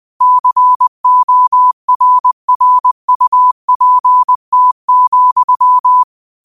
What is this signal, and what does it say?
Morse code, heard as a keyed tone, about 20 wpm, 1000 Hz. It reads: CORRUPT,
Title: 20 WPM morse code quotes for Thu, 29 Feb 2024